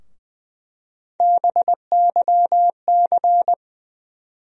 Morse code, heard as a keyed tone, about 20 wpm, 700 Hz. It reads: BYC
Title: USA Call Sign Postfixes - 3 Letters 20wpm